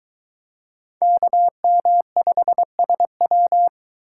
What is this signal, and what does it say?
Morse code, heard as a keyed tone, about 23 wpm, 700 Hz. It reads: KM5SW